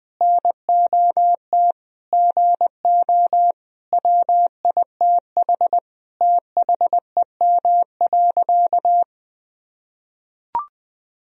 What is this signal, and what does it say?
Morse code, heard as a keyed tone, about 20 wpm, 700 Hz. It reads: NOT GO WITH THEM.
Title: Sentences from Top 1000 Words - Review Part 4 - 20wpm